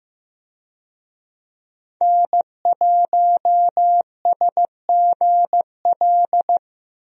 Morse code, heard as a keyed tone, about 16 wpm, 700 Hz. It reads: N1SGL
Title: USA Call Signs - 5 Characters - 15wpm